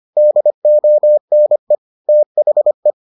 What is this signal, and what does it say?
Morse code, heard as a keyed tone, about 25 wpm, 600 Hz. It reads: DONE THE